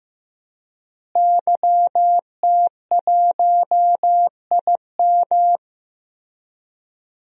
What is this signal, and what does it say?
Morse code, heard as a keyed tone, about 15 wpm, 700 Hz. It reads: YT1IM
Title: DXCC-100 Call Signs - 5 Characters - Repeated 3x - 15wpm